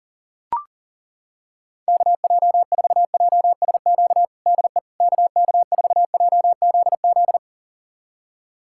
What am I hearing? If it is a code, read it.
Morse code, heard as a keyed tone, about 40 wpm, 700 Hz. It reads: KJ4JSQ DE KK4JZZ
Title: USA Callsign DE Callsign 40wpm